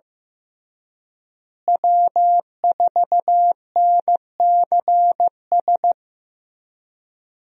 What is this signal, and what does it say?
Morse code, heard as a keyed tone, about 15 wpm, 700 Hz. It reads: W4NCS
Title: USA Call Signs - 5 Characters - Repeated 3x - 15wpm